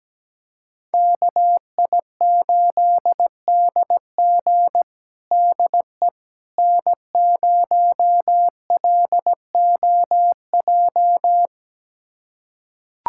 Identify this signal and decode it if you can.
Morse code, heard as a keyed tone, about 17 wpm, 700 Hz. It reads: KI8DG DE N0LOJ
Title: USA Callsign DE Callsign - 5 Characters - 17wpm